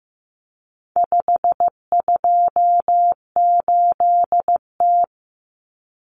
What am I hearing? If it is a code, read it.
Morse code, heard as a keyed tone, about 15 wpm, 700 Hz. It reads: 528T
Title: Common Cut Numbers - 4-Digit 15wpm